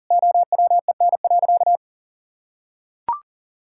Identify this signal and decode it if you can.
Morse code, heard as a keyed tone, about 37 wpm, 700 Hz. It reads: OWEN.
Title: Pangram Challenge v2 - 40wpm